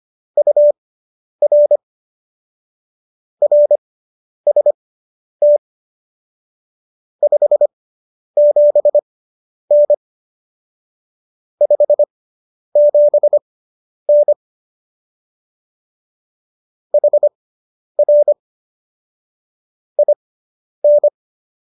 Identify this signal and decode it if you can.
Morse code, heard as a keyed tone, about 25 wpm, spaced as for 10 wpm, 590 Hz. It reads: UR RST 57N 57N HR IN
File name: QSO220 10